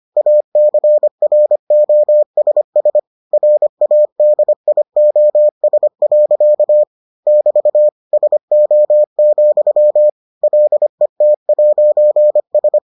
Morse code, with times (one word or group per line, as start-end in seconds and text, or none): 0.2-3.0 ACROSS
3.3-6.8 RADIOS.
7.3-7.9 <BT>
8.1-10.1 SO,
10.4-12.8 LET'S